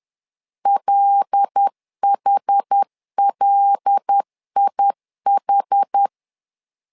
Morse code, heard as a keyed tone, about 10 wpm, 800 Hz. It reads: LHLIH
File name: LEZIONI / 5 LBVC / 5 L 10wpm 2259